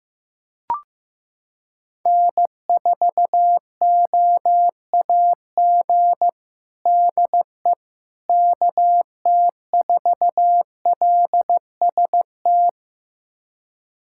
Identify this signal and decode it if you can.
Morse code, heard as a keyed tone, about 15 wpm, 700 Hz. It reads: N4OAG DE KT4LST
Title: USA Callsign DE Callsign 15wpm